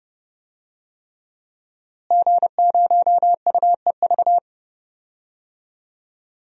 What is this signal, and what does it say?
Morse code, heard as a keyed tone, about 32 wpm, 700 Hz. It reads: G0UEV
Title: DXCC-100 Call Signs - 5 Characters - Repeated 3x - 30wpm